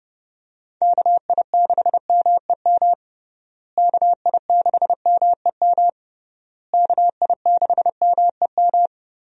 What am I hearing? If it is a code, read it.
Morse code, heard as a keyed tone, about 30 wpm, 700 Hz. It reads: KI6MEM KI6MEM KI6MEM
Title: USA Call Signs - 6 Characters - Repeated 3x - 30wpm